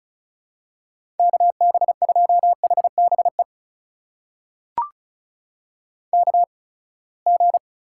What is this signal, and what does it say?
Morse code, heard as a keyed tone, about 36 wpm, spaced as for 24 wpm, 700 Hz. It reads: KB2HBE KG
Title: USA Call Signs - 6 Characters - 8x Character Spacing - 35wpm